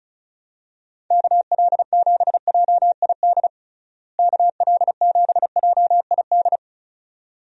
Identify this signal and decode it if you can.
Morse code, heard as a keyed tone, about 36 wpm, 700 Hz. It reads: KL7JID KL7JID
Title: DXCC-100 Call Signs - 6 Characters - Repeated 2x - 35wpm